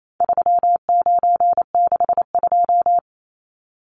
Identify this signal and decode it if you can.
Morse code, heard as a keyed tone, about 28 wpm, 700 Hz. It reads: 3962
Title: Numbers - 4-Digit 28wpm